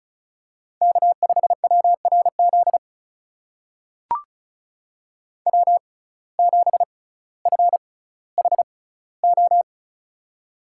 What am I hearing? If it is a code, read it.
Morse code, heard as a keyed tone, about 36 wpm, spaced as for 18 wpm, 700 Hz. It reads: K5WRZ W7FHO
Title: USA Call Signs - 5 Characters - 6x Character Spacing - 35wpm